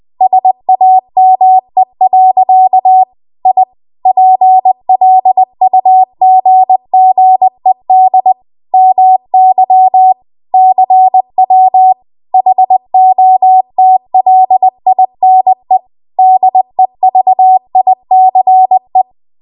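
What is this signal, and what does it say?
Morse code, heard as a keyed tone, about 20 wpm, 750 Hz. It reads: SAME. I PLUGGED MY CW HOTLINE DEVICE
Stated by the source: Morse code practice at 20 WPM.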